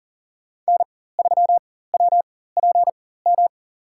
Morse code, heard as a keyed tone, about 40 wpm, spaced as for 19 wpm, 700 Hz. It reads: N3WPM
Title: USA Call Signs - 4x Character Spacing - 40wpm